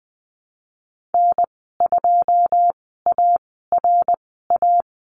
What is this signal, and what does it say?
Morse code, heard as a keyed tone, about 20 wpm, spaced as for 15 wpm, 700 Hz. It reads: N2ARA
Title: 088 Lesson 15g - Number 5 - Call Signs 20wpm